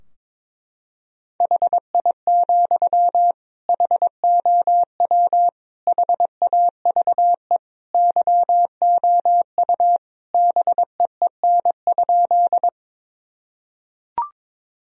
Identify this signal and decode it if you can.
Morse code, heard as a keyed tone, about 22 wpm, 700 Hz. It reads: HI, HOW HAVE YOU BEEN?
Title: Sentences from Top 1000 Words - Review Part 3 - 22wpm